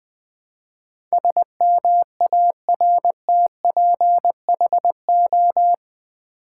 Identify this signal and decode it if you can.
Morse code, heard as a keyed tone, about 20 wpm, 700 Hz. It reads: SMARTPHO
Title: Build A Word - 4 Letters At A Time - 10-Letter Words - 20wpm